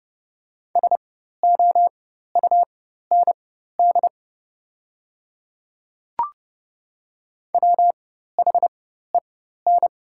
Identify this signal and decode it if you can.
Morse code, heard as a keyed tone, about 30 wpm, spaced as for 14 wpm, 700 Hz. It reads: SOUND WHEN
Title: 4x Character Spacing - Top 100 Words - 30wpm